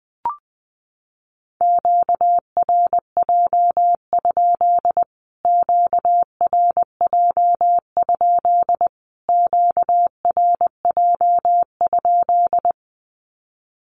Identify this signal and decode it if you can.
Morse code, heard as a keyed tone, about 20 wpm, 700 Hz. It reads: QRJ? QRJ? QRJ?
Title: Q-Codes Repeated 3x 20wpm